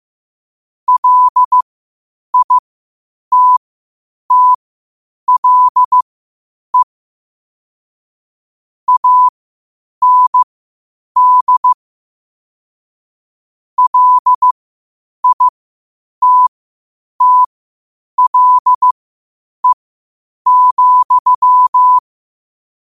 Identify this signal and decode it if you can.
Morse code, heard as a keyed tone, about 15 wpm, spaced as for 8 wpm, 1000 Hz. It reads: LITTLE AND LITTLE,
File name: qotdfiles / 8wpm